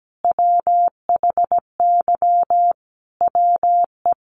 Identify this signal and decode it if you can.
Morse code, heard as a keyed tone, about 17 wpm, 700 Hz. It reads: WHY WE
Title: Sentences from Top 1100 Words 17wpm